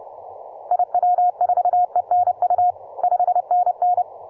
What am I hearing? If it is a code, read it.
Morse code, heard as a keyed tone, about 29 wpm, 720 Hz. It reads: IW4ENU 5NN